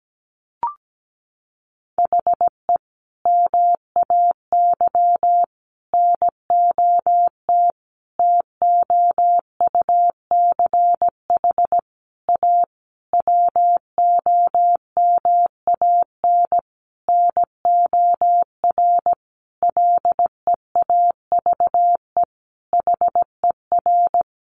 Morse code, heard as a keyed tone, about 17 wpm, 700 Hz. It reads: HE MAY NOT TOUCH A WOMAN NOR LEAVE HER
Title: Sentences from Top 1500 Words 17wpm